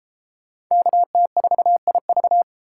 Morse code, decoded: KT4IV

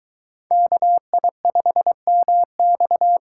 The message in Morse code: KI5MX

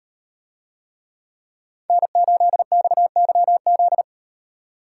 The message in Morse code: N8XYZ

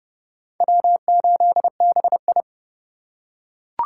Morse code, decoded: W8BI

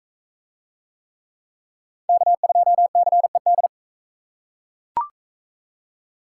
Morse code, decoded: K2CED

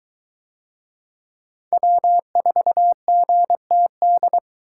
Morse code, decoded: W4GTD

